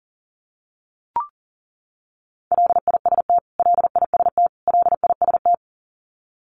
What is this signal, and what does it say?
Morse code, heard as a keyed tone, about 38 wpm, 700 Hz. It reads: LIST LIST LIST
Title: Top 500 Words Repeated 40wpm